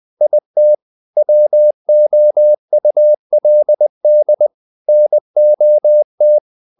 Morse code, heard as a keyed tone, about 20 wpm, 600 Hz. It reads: IT WOULD NOT
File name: the old man and the sea part 2 20wpm